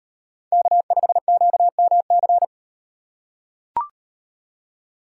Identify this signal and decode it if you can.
Morse code, heard as a keyed tone, about 38 wpm, 700 Hz. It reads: K5QMC